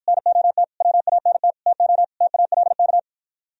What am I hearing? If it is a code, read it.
Morse code, heard as a keyed tone, about 52 wpm, 700 Hz. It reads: NOT WANT TO TALK